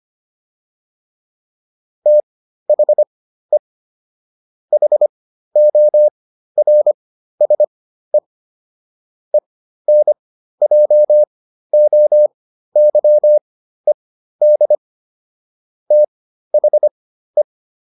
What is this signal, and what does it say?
Morse code, heard as a keyed tone, about 25 wpm, spaced as for 13 wpm, 600 Hz. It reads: THE HORSE ENJOYED THE